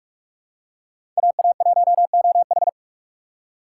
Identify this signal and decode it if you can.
Morse code, heard as a keyed tone, about 42 wpm, 700 Hz. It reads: AA1OH
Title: USA Call Signs - 4x Character Spacing - 45wpm